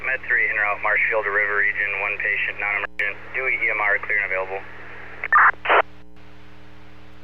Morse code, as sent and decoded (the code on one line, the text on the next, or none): none